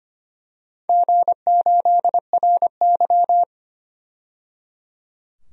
Morse code, heard as a keyed tone, about 25 wpm, 700 Hz. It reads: G8RY
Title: DXCC-100 Call Signs - 4 Characters - 25wpm